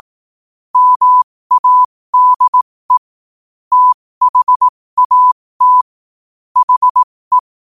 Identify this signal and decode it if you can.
Morse code, heard as a keyed tone, about 18 wpm, spaced as for 15 wpm, 1000 Hz. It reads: MADE THAT HE
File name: qotdfiles / 15wpm